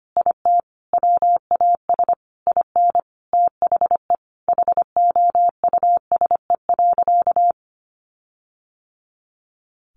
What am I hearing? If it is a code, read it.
Morse code, heard as a keyed tone, about 25 wpm, 700 Hz. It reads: IT WAS IN THE HOUSE.
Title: Sentences from Top 1500 Words 25wpm